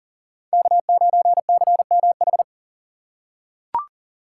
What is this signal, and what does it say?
Morse code, heard as a keyed tone, about 40 wpm, 700 Hz. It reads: K9CMH
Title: USA Call Signs - 5 Characters - 3x Character Spacing - 40wpm